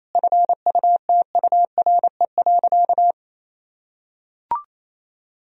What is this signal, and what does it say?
Morse code, heard as a keyed tone, about 27 wpm, 700 Hz. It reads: FUTURE.
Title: Sentences from Top 1000 Words - Review Part 4 - 28wpm